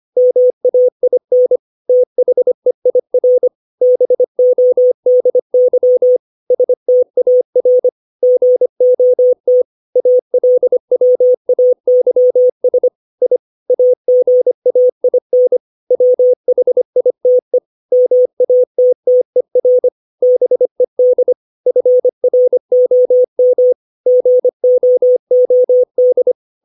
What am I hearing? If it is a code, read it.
Morse code, heard as a keyed tone, about 25 wpm, 500 Hz. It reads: MAIN THEIR BODY STAR GOT ALWAYS I AGAIN WHITE MATTER BED FROM GOOD
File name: 632 words / 632 Words 25WPM